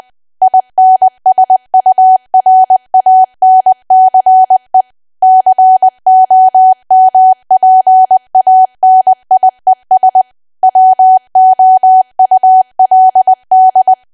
Morse code, heard as a keyed tone, about 20 wpm, 750 Hz. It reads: INSURANCE COMPANIES WOULD
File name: cwa / W1AW-Level3 / 20WPM